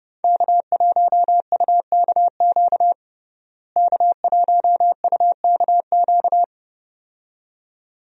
Morse code, heard as a keyed tone, about 30 wpm, 700 Hz. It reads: K1UKQ K1UKQ
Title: USA Call Signs - 5 Characters - Repeated 2x - 30wpm